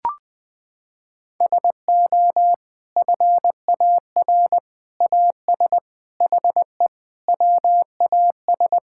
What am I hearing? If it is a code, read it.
Morse code, heard as a keyed tone, about 20 wpm, 700 Hz. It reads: SO FAR AS HE WAS
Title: Sentences from Top 1000 Words - Review Part 6 - 20wpm